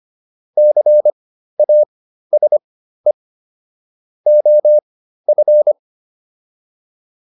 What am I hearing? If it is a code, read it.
Morse code, heard as a keyed tone, about 25 wpm, spaced as for 13 wpm, 600 Hz. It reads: CASE OF
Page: /intermediate-practice-files/